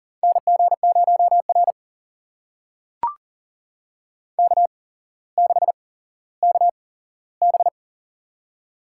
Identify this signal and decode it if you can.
Morse code, heard as a keyed tone, about 38 wpm, spaced as for 17 wpm, 700 Hz. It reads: NG0R K6KB
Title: USA Call Signs - 8x Character Spacing - 40wpm